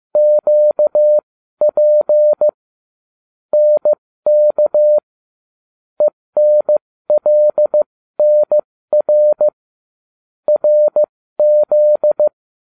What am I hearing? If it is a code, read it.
Morse code, heard as a keyed tone, about 15 wpm, spaced as for 13 wpm, 600 Hz. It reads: QP NK ENLNR RZ